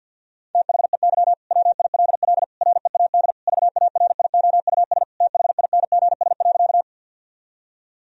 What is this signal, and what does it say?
Morse code, heard as a keyed tone, about 50 wpm, 700 Hz. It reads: THEY WILL READ VARIOUS THINGS.
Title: Sentences from Top 1000 Words - Review Part 4 - 50wpm